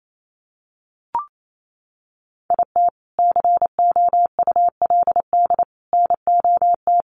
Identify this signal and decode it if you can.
Morse code, heard as a keyed tone, about 28 wpm, 700 Hz. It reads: IT COULD NOT
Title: Sentences from Top 1200 Words 28wpm